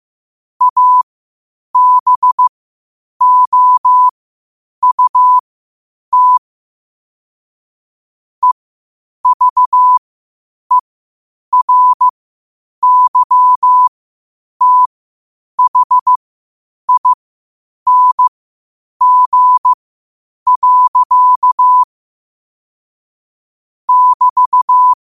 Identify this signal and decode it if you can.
Morse code, heard as a keyed tone, about 15 wpm, spaced as for 8 wpm, 1000 Hz. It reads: ABOUT EVERYTHING. <BT>
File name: qotdfiles / 8wpm